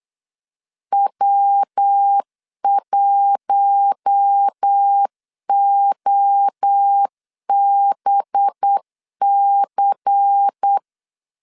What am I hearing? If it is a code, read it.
Morse code, heard as a keyed tone, about 8 wpm, 800 Hz. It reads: W1OBC
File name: LEZIONI / 6 PKX0 / 6 PK 8wpm 3003